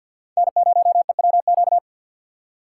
Morse code, heard as a keyed tone, about 45 wpm, 700 Hz. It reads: N0EWQ
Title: USA Call Signs - 5 Characters - 6x Character Spacing - 50wpm